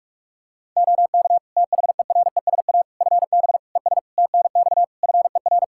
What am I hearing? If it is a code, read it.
Morse code, heard as a keyed tone, about 46 wpm, 700 Hz. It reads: OK THERESA FB ES TNX FER
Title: Practice QSO - Exchange 2 - Info - 45wpm